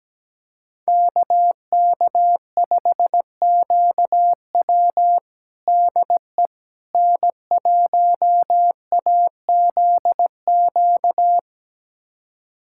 Morse code, heard as a keyed tone, about 17 wpm, 700 Hz. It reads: KK5QW DE N1AZQ
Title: USA Callsign DE Callsign - 5 Characters - 17wpm